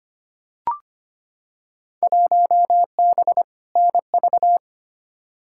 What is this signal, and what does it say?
Morse code, heard as a keyed tone, about 25 wpm, 700 Hz. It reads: 1B NV
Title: Field Day Exchanges - 25wpm